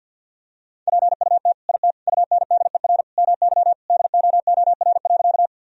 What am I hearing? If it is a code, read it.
Morse code, heard as a keyed tone, about 49 wpm, 700 Hz. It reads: PUT IT UNDER MY DOOR.
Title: Sentences from Top 1000 Words - Review Part 1 - 50wpm